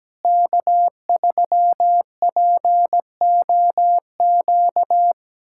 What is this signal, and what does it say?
Morse code, heard as a keyed tone, about 17 wpm, 700 Hz. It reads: K3POQ
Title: USA Callsign DE Callsign - 5 Characters - 17wpm